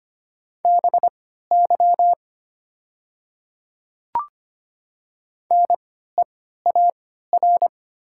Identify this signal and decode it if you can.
Morse code, heard as a keyed tone, about 25 wpm, spaced as for 14 wpm, 700 Hz. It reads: BY NEAR
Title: 3x Character Spacing - Top 300 Words - 25wpm